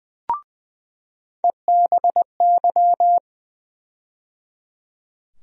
Morse code, heard as a keyed tone, about 20 wpm, 700 Hz. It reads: EBY
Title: USA Call Sign Postfixes - 3 Letters 20wpm